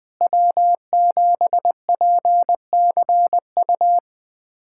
Morse code, heard as a keyed tone, about 21 wpm, 700 Hz. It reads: W7PCU
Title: USA Callsign DE Callsign - 5 Characters - 20wpm